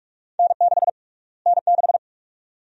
Morse code, heard as a keyed tone, about 46 wpm, 700 Hz. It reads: N6 N6
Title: USA Call Sign Prefixes - Repeated 2x - 45wpm